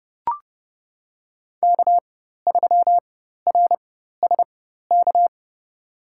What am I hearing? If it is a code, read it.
Morse code, heard as a keyed tone, about 30 wpm, spaced as for 14 wpm, 700 Hz. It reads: K3RSK
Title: USA Call Signs - 5 Characters - 4x Character Spacing - 30wpm